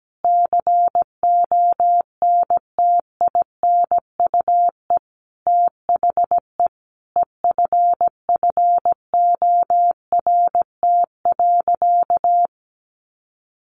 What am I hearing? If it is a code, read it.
Morse code, heard as a keyed tone, about 17 wpm, 700 Hz. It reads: CONTINUE THE EFFORT.
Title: Sentences from Top 1200 Words 17wpm